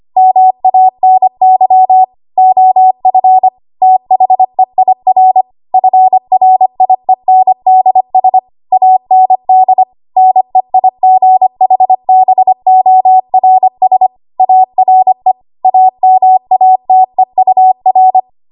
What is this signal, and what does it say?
Morse code, heard as a keyed tone, about 25 wpm, 750 Hz. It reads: MANY OF THEIR FRIENDS AND NEIGHBORS ARE AMATEUR
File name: Morse / 25WPM